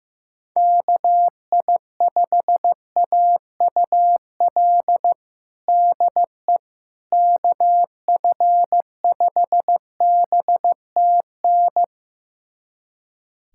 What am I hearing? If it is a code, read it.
Morse code, heard as a keyed tone, about 15 wpm, 700 Hz. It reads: KI5AUL DE KF5BTN
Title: USA Callsign DE Callsign - 6 Characters - 15wpm